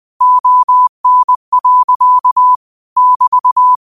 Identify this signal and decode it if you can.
Morse code, heard as a keyed tone, about 20 wpm, 1000 Hz. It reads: ON. <BT>
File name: qotdfiles / 20wpm